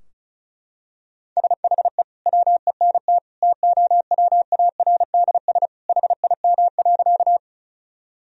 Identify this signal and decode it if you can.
Morse code, heard as a keyed tone, about 36 wpm, 700 Hz. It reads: SHE WENT TOWARDS HIM.
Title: Sentences from Top 1000 Words - Review Part 2 - 35wpm